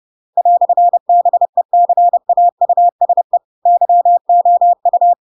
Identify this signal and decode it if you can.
Morse code, heard as a keyed tone, about 30 wpm, 700 Hz. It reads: "BECAUSE YOU
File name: Pinocchio / Pinocchio ch4 30 30 ik1hge